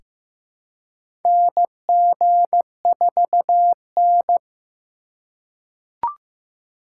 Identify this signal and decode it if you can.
Morse code, heard as a keyed tone, about 15 wpm, 700 Hz. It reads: NG4N
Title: USA Call Signs - 4 Characters - 4x Character Spacing - 15wpm